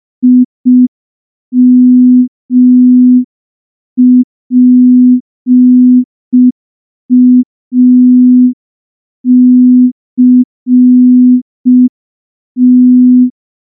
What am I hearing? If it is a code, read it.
Morse code, heard as a keyed tone, about 5 wpm, 250 Hz. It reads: IMPACT
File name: morse / morse-training-05WPM